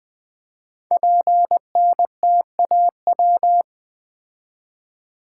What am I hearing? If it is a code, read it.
Morse code, heard as a keyed tone, about 20 wpm, 700 Hz. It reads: PNTAW
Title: Code Groups - 5 Letters - 20wpm